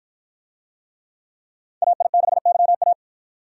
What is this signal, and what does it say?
Morse code, heard as a keyed tone, about 53 wpm, 700 Hz. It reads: AI6YA